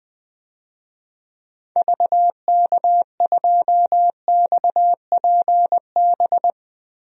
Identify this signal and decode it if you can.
Morse code, heard as a keyed tone, about 20 wpm, 700 Hz. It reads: VK2XPB
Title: DXCC-100 Call Signs - 6 Characters - 20wpm